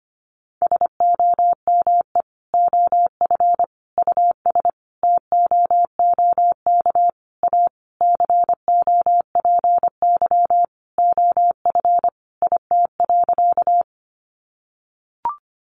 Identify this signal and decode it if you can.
Morse code, heard as a keyed tone, about 25 wpm, 700 Hz. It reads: SOME OF US TOOK A COPY OF IT.
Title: Sentences from Top 1400 Words 25wpm